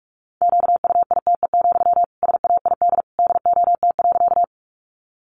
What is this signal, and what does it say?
Morse code, heard as a keyed tone, about 44 wpm, 700 Hz. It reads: QUITE, SAID DOT.
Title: Sentences from Top 1500 Words 45wpm